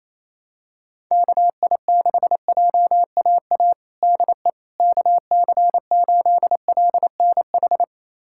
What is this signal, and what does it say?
Morse code, heard as a keyed tone, about 28 wpm, 700 Hz. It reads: KI6JAA DE KC8LNH
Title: USA Callsign DE Callsign 28wpm